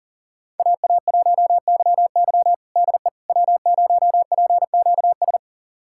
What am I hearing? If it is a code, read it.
Morse code, heard as a keyed tone, about 40 wpm, 700 Hz. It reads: AA1YY DE W0PQS
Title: USA Callsign DE Callsign - 5 Characters - 40wpm